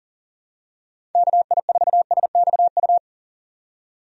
Morse code, heard as a keyed tone, about 39 wpm, 700 Hz. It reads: KI4SXU
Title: USA Call Signs - 6 Characters - 40wpm